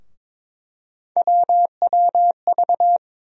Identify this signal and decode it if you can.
Morse code, heard as a keyed tone, about 22 wpm, 700 Hz. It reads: WWV
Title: USA Call Sign Postfixes - 3 Letters 22wpm